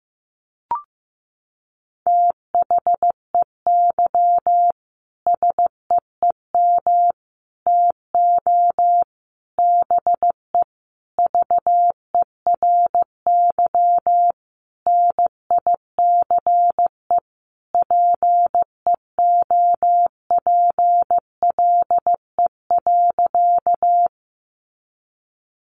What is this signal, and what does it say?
Morse code, heard as a keyed tone, about 15 wpm, 700 Hz. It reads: THEY SEEM TO BE VERY NICE PEOPLE.
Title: Sentences from Top 1100 Words 15wpm